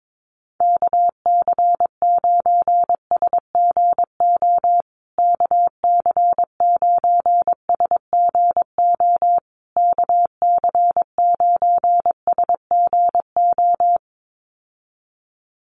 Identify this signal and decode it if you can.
Morse code, heard as a keyed tone, about 22 wpm, 700 Hz. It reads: KC9SGO KC9SGO KC9SGO
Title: USA Call Signs Repeated 3x 22wpm